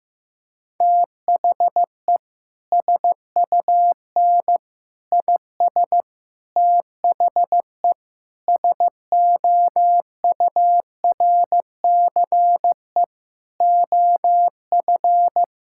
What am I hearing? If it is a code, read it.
Morse code, heard as a keyed tone, about 15 wpm, 700 Hz. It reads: THE SUN IS THE SOURCE OF
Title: Sentences from Top 1000 Words - Review Part 5 - 15wpm